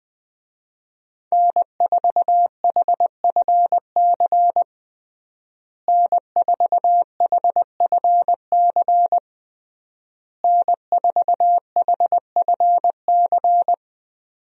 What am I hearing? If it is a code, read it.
Morse code, heard as a keyed tone, about 20 wpm, 700 Hz. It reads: N4HFC N4HFC N4HFC
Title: USA Call Signs - 5 Characters - Repeated 3x - 20wpm